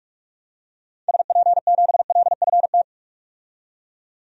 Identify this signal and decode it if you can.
Morse code, heard as a keyed tone, about 45 wpm, 700 Hz. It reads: SP7LFT
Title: DXCC-100 Call Signs - 6 Characters - 45wpm